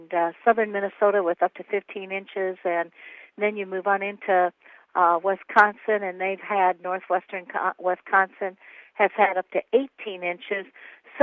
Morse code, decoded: none

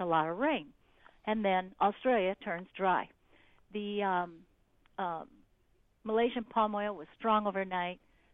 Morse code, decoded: none

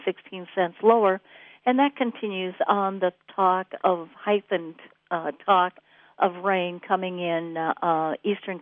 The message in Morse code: none

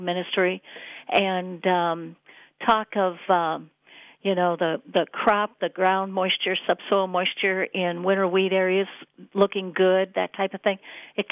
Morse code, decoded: none

